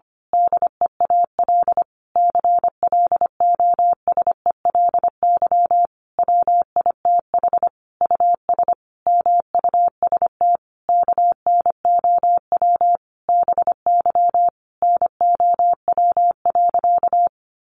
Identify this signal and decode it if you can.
Morse code, heard as a keyed tone, about 25 wpm, 700 Hz. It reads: DEAL CLOSELY WITH US MUST KNOW BY NOW.
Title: Sentences from Top 1400 Words 25wpm